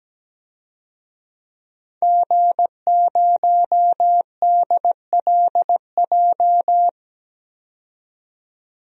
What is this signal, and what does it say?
Morse code, heard as a keyed tone, about 17 wpm, 700 Hz. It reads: G0DLJ
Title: DXCC-100 Call Signs - 5 Characters - 17wpm